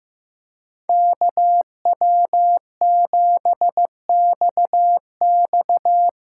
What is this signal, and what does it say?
Morse code, heard as a keyed tone, about 15 wpm, 700 Hz. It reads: KW7XX
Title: USA Call Signs - 5 Characters - 8x Character Spacing - 15wpm